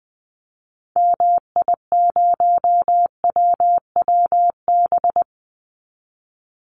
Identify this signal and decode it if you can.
Morse code, heard as a keyed tone, about 20 wpm, 700 Hz. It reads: MI0WWB
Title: Call Signs - 20wpm